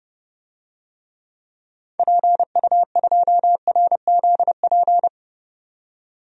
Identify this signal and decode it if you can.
Morse code, heard as a keyed tone, about 30 wpm, 700 Hz. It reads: PU2RZP
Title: DXCC-100 Call Signs - 6 Characters - 30wpm